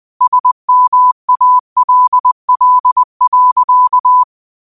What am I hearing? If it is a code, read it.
Morse code, heard as a keyed tone, about 20 wpm, 1000 Hz. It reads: SMALL.